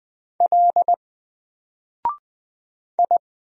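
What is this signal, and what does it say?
Morse code, heard as a keyed tone, about 20 wpm, 700 Hz. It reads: L I
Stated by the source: Single letter and number Morse drill at 20 wpm, from N4LCD.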